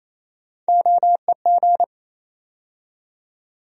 Morse code, heard as a keyed tone, about 28 wpm, 700 Hz. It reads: OEG
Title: USA Call Sign Postfixes - 3 Letters 28wpm